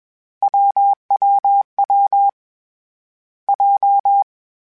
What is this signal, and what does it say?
Morse code, heard as a keyed tone, about 21 wpm, 800 Hz. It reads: WWW J
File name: CW Code Groups-mp3s / Sets mp3s / AUV sp @20